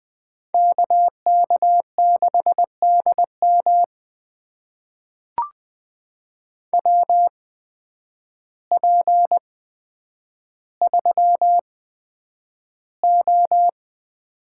Morse code, heard as a keyed tone, about 20 wpm, spaced as for 9 wpm, 700 Hz. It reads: KK6DM WP3O
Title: USA Call Signs - 5 Characters - 8x Character Spacing - 20wpm